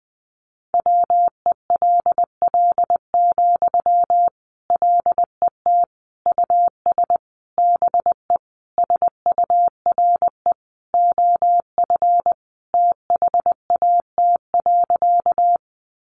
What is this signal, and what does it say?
Morse code, heard as a keyed tone, about 20 wpm, 700 Hz. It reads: WELL, LET US BE SURE OF THAT.
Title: Sentences from Top 300 Words 20wpm